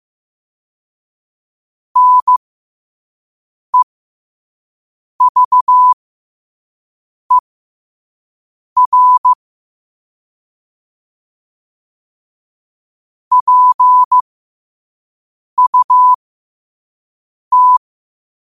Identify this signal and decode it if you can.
Morse code, heard as a keyed tone, about 15 wpm, spaced as for 5 wpm, 1000 Hz. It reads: NEVER PUT
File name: qotdfiles / 5wpm